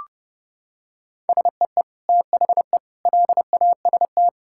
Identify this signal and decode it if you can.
Morse code, heard as a keyed tone, about 30 wpm, 700 Hz. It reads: SEE THE LAST
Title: Sentences from Top 1000 Words - Review Part 5 - 30wpm